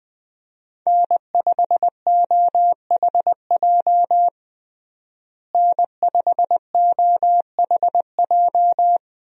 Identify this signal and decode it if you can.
Morse code, heard as a keyed tone, about 21 wpm, 700 Hz. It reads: N5OHJ N5OHJ
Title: USA Call Signs - 5 Characters - Repeated 3x - 20wpm